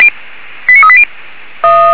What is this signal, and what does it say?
Five-tone selective call (ZVEI), heard as quick runs of five tones, tones of about 70 ms. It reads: 89380